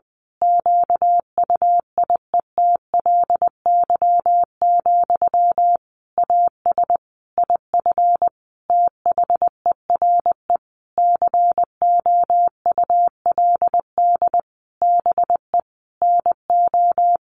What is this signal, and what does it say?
Morse code, heard as a keyed tone, about 20 wpm, 700 Hz. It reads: QUIETLY, AS IF THERE COULD BE NO
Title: A Sound Of Thunder 20wpm